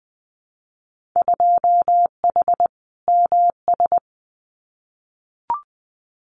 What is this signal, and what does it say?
Morse code, heard as a keyed tone, about 20 wpm, 700 Hz. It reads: 2H MS
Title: Winter Field Day Exchanges 20wpm